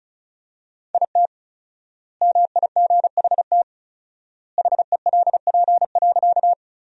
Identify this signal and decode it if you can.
Morse code, heard as a keyed tone, about 36 wpm, 700 Hz. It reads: IT MIGHT HELP.
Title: 4x Word-Spacing - Sentences from Top 300 Words - 35wpm